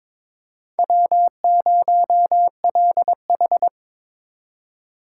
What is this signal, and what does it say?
Morse code, heard as a keyed tone, about 25 wpm, 700 Hz. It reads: W0LH
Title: Speed Racing - USA 1x2 Call Signs 22wpm